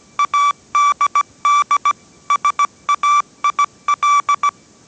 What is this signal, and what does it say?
Morse code, heard as a keyed tone, about 17 wpm, 1160 Hz. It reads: ADD SAIL